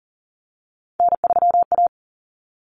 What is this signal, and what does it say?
Morse code, heard as a keyed tone, about 40 wpm, 700 Hz. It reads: N3A
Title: Common Cut Numbers - 3-Digit 40wpm